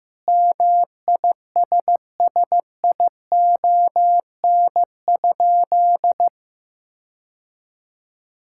Morse code, heard as a keyed tone, about 15 wpm, 700 Hz. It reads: MISSION?
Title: Sentences from Top 1600 Words 15wpm